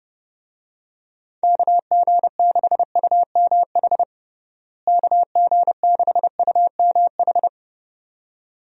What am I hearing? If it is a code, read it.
Morse code, heard as a keyed tone, about 30 wpm, 700 Hz. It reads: KG6UMH KG6UMH
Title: USA Call Signs - 6 Characters - Repeated 2x - 30wpm